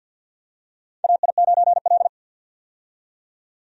Morse code, heard as a keyed tone, about 52 wpm, 700 Hz. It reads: AI9L
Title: USA Call Signs - 4 Characters - 50wpm